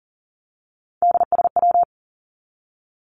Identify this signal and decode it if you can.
Morse code, heard as a keyed tone, about 40 wpm, 700 Hz. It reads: DSW
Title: Abbreviations 40wpm